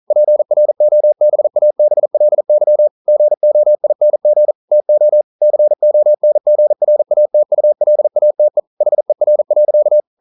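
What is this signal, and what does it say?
Morse code, heard as a keyed tone, about 42 wpm, 600 Hz. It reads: PROBABLY GOING TO CONGRATULATE HER.
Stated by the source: Morse code practice at 41 wpm, from CW med Gnister.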